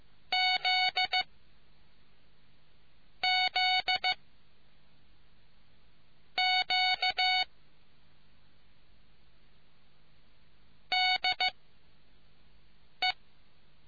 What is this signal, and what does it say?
Morse code, heard as a keyed tone, about 15 wpm, 740 Hz. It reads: Z Z Q D E